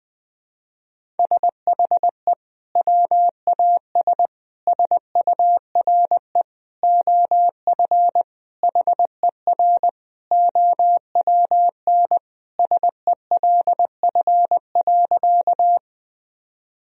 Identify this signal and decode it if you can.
Morse code, heard as a keyed tone, about 20 wpm, 700 Hz. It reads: SHE WAS SURE OF HER OWN SELF.
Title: Sentences from Top 1000 Words - Review Part 1 - 20wpm